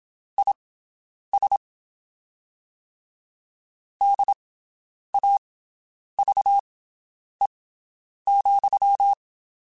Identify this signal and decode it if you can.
Morse code, heard as a keyed tone, about 27 wpm, spaced as for 9 wpm, 800 Hz. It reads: IS DAVE,
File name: (spaced) / QSOs Spaced-mp3s / WD0BGP 1 @25 sp